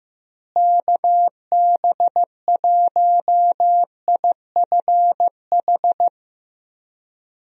Morse code, heard as a keyed tone, about 15 wpm, 700 Hz. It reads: KB1IFH
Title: USA Call Signs - 6 Characters - Repeated 3x - 15wpm